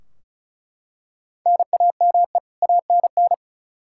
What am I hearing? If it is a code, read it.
Morse code, heard as a keyed tone, about 34 wpm, 700 Hz. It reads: NAME ANN
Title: K1USN SST QSO Practice 35wpm